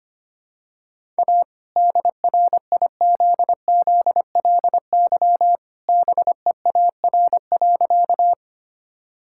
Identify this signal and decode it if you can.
Morse code, heard as a keyed tone, about 25 wpm, 700 Hz. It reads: A DRIZZLY BEAR.
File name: practice / 25wpm / All-Age Appropriate Jokes - 25wpm